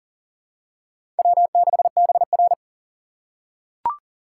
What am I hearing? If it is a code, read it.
Morse code, heard as a keyed tone, about 42 wpm, 690 Hz. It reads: W6BR
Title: USA Call Signs - 4 Characters - 2x Character Spacing - 40wpm